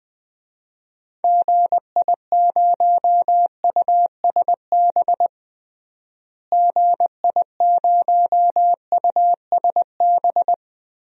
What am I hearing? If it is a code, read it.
Morse code, heard as a keyed tone, about 20 wpm, 700 Hz. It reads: GI0USB GI0USB
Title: DXCC-100 Call Signs - 6 Characters - Repeated 3x - 20wpm